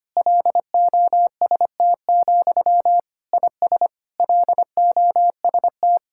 Morse code, heard as a keyed tone, about 25 wpm, 700 Hz. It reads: LOST, IS LOST
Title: Sentences from Top 1600 Words 25wpm